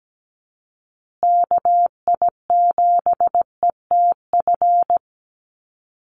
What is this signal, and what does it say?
Morse code, heard as a keyed tone, about 17 wpm, 700 Hz. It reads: KI7ETF
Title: USA Call Signs Repeated 3x 17wpm